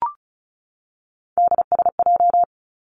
Morse code, heard as a keyed tone, about 36 wpm, 700 Hz. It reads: DSJ
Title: Three Challenging Letter-Number Combinations 35wpm